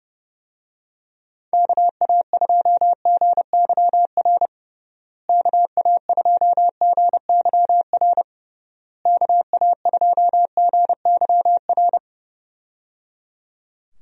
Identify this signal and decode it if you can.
Morse code, heard as a keyed tone, about 29 wpm, 700 Hz. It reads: KA2GYR KA2GYR KA2GYR
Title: USA Call Signs - 6 Characters - Repeated 3x - 30wpm